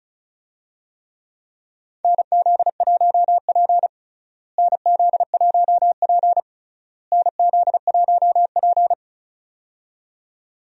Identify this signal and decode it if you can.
Morse code, heard as a keyed tone, about 34 wpm, 700 Hz. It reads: NZ1P NZ1P NZ1P
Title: USA Call Signs - 4 Characters - Repeated 3x - 35wpm